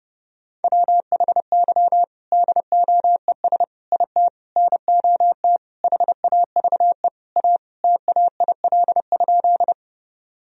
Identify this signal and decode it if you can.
Morse code, heard as a keyed tone, about 32 wpm, 700 Hz. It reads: WHY DOES IT NOT HAVE A TAIL?
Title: Sentences from Top 1000 Words - Review Part 3 - 30wpm